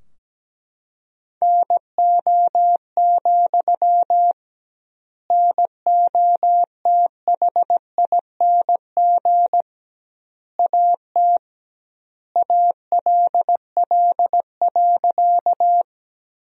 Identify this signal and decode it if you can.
Morse code, heard as a keyed tone, about 17 wpm, 700 Hz. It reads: NO, NOTHING AT ALL.
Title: 2x Word-Spacing - Sentences from Top 500 Words - 17wpm